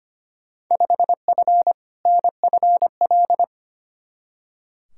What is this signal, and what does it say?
Morse code, heard as a keyed tone, about 25 wpm, 700 Hz. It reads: 5F NFL
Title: Field Day Exchanges Encore - 25wpm